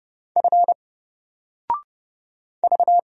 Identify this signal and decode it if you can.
Morse code, heard as a keyed tone, about 30 wpm, 700 Hz. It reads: F V
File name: Single-Letter-Number-30wpm